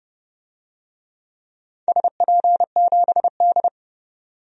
Morse code, heard as a keyed tone, about 30 wpm, 700 Hz. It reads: SP7D
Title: DXCC-100 Call Signs - 4 Characters - 30wpm